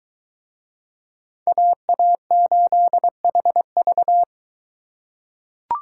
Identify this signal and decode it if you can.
Morse code, heard as a keyed tone, about 22 wpm, 700 Hz. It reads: AA8HV